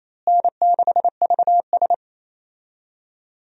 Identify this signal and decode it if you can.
Morse code, heard as a keyed tone, about 28 wpm, 700 Hz. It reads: N6VS
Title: USA Call Signs - 4 Characters - Repeated 3x - 28wpm